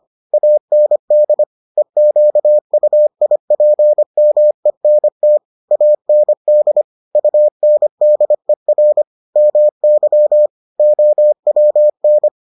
Morse code, heard as a keyed tone, about 25 wpm, 600 Hz. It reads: AND EQUIPMENT AND UNDER MY OWN